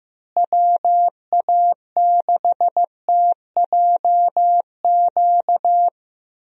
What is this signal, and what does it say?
Morse code, heard as a keyed tone, about 14 wpm, 700 Hz. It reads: WA6TJQ
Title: USA Call Signs - 6 Characters - Repeated 3x - 15wpm